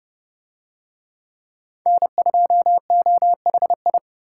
Morse code, heard as a keyed tone, about 28 wpm, 700 Hz. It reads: N2OHI